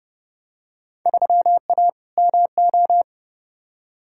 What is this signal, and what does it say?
Morse code, heard as a keyed tone, about 30 wpm, 700 Hz. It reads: 3A MO
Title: Field Day Exchanges Encore - 30wpm